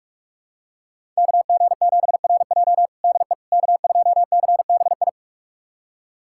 Morse code, heard as a keyed tone, about 46 wpm, 700 Hz. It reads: KG7RJ DE K2CBI
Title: USA Callsign DE Callsign - 5 Characters - 45wpm